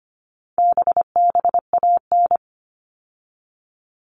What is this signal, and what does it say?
Morse code, heard as a keyed tone, about 25 wpm, 700 Hz. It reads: BBAN
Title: All Cut Numbers - 4-Digit 25wpm